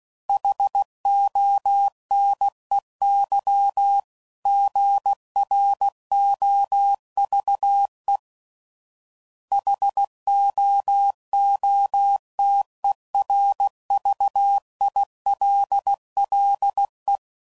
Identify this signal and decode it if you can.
Morse code, heard as a keyed tone, about 16 wpm, 800 Hz. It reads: HONEY GROVE HOOTERVILLE
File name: CW Code Groups @15WPM / fromSM's / Cities